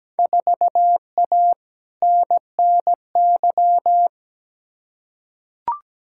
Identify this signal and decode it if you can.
Morse code, heard as a keyed tone, about 17 wpm, 700 Hz. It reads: 4A NNY
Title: Field Day Exchanges - 17wpm